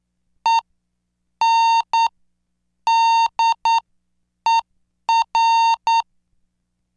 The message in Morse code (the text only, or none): ENDER